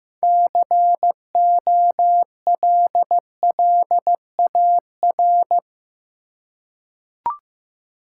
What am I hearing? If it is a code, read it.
Morse code, heard as a keyed tone, about 15 wpm, 700 Hz. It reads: COLLAR
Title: Six Letter Words - 15wpm